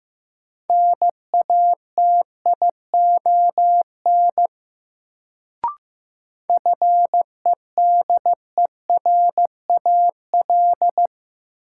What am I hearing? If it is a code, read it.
Morse code, heard as a keyed tone, about 15 wpm, 700 Hz. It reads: NATION FEDERAL
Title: Sets of 3 Words - New Word Drill - 15wpm